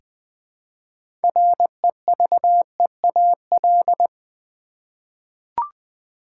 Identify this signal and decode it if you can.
Morse code, heard as a keyed tone, about 20 wpm, 700 Hz. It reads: REVEAL
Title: Six Letter Words - 20wpm